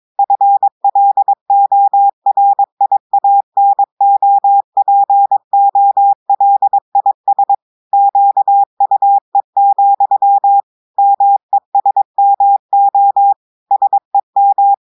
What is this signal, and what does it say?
Morse code, heard as a keyed tone, about 22 wpm, 800 Hz. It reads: FLORIANOPOLIS QUE, MESMO SEM
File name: audios / portugais / portugais texte en clair 1 vitesse 22 mots minutes